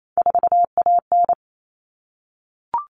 Morse code, heard as a keyed tone, about 28 wpm, 700 Hz. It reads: 4AN
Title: Common Cut Numbers - 3-Digit 28wpm